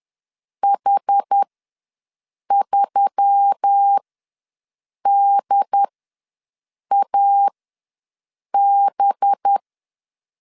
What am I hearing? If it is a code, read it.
Morse code, heard as a keyed tone, about 10 wpm, spaced as for 6 wpm, 800 Hz. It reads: H3DAB
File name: LEZIONI / 7 GF9Z / 7 GF ws 10wpm 3918